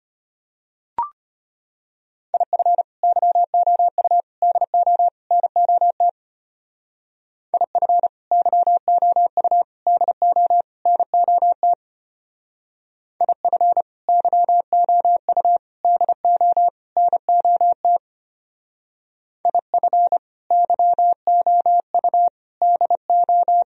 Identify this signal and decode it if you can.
Morse code, heard as a keyed tone, about 31 wpm, 700 Hz. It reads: IF YOU DO NOT IF YOU DO NOT IF YOU DO NOT IF YOU DO
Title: Speed-Racing - Sets of 4 Words 25wpm